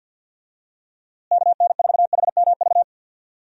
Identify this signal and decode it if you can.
Morse code, heard as a keyed tone, about 52 wpm, 700 Hz. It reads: KN4HMV
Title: USA Call Signs - 6 Characters - Repeated 3x - 50wpm